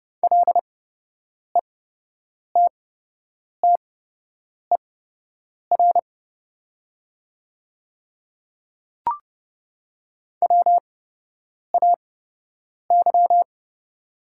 Morse code, heard as a keyed tone, about 30 wpm, spaced as for 8 wpm, 700 Hz. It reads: LETTER WAY
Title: 8x Character Spacing - Top 300 Words - 30wpm